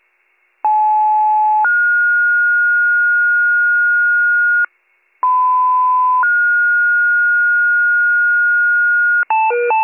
Two-tone page, heard as a one-second tone, then a three-second tone, about 860 then 1430 Hz, about 1010 then 1430 Hz. It is heard from 0.6 to 9.2 s.